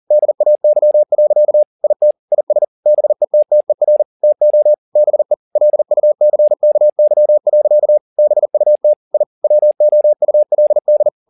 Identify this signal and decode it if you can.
Morse code, heard as a keyed tone, about 40 wpm, 600 Hz. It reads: DAY. IT IS BETTER TO BE LUCKY. BUT I WOULD